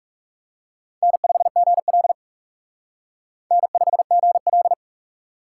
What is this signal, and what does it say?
Morse code, heard as a keyed tone, about 36 wpm, 700 Hz. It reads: N5GL N5GL
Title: Speed Racing - USA Call Signs 30wpm